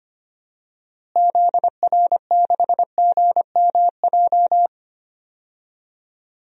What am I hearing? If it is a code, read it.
Morse code, heard as a keyed tone, about 25 wpm, 700 Hz. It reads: ZR6GMJ
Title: DXCC-100 Call Signs - 6 Characters - Repeated 3x - 25wpm